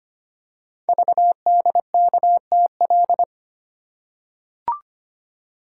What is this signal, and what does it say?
Morse code, heard as a keyed tone, about 25 wpm, 700 Hz. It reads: VDKTL
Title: Code Groups - 5 Letters - 25wpm